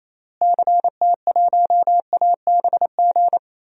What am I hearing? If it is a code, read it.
Morse code, heard as a keyed tone, about 27 wpm, 700 Hz. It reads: CT1ABG
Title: DXCC-100 Call Signs - 6 Characters - Repeated 2x - 28wpm